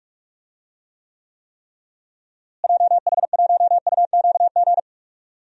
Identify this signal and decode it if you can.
Morse code, heard as a keyed tone, about 46 wpm, 700 Hz. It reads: JH1UQG
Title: DXCC-100 Call Signs - 6 Characters - Repeated 3x - 45wpm